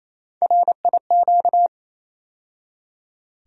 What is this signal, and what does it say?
Morse code, heard as a keyed tone, about 28 wpm, 700 Hz. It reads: RIQ
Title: USA Call Sign Postfixes - 3 Letters 28wpm